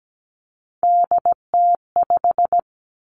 Morse code, heard as a keyed tone, about 17 wpm, 700 Hz. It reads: DT5
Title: Three Challenging Letter-Number Combinations 17wpm